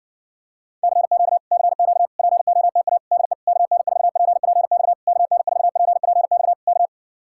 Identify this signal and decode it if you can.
Morse code, heard as a keyed tone, about 60 wpm, 700 Hz. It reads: CQ CQ POTA DE KN4PPX KN4PPX K